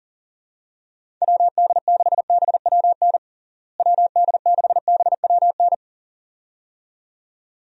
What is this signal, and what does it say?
Morse code, heard as a keyed tone, about 40 wpm, 700 Hz. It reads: WD6BWN WD6BWN
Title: USA Call Signs - 6 Characters - Repeated 2x - 40wpm